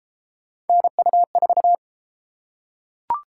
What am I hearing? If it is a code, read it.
Morse code, heard as a keyed tone, about 32 wpm, 700 Hz. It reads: NU4